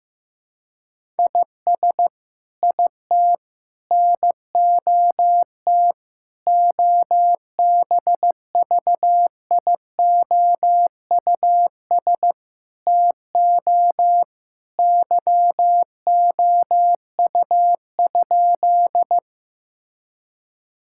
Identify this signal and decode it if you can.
Morse code, heard as a keyed tone, about 16 wpm, 700 Hz. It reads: IS IT NOT OBVIOUS TO YOU?
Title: Sentences from Top 1600 Words 15wpm